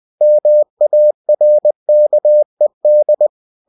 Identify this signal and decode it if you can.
Morse code, heard as a keyed tone, about 20 wpm, 600 Hz. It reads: MARKED
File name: cw-books / winnie-the-pooh / winnie-20wpm-page-0109